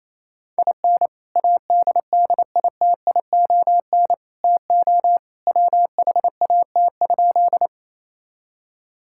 Morse code, heard as a keyed tone, about 28 wpm, 700 Hz. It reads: IN ADDITION TO WHAT?
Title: Sentences from Top 1000 Words - Review Part 6 - 28wpm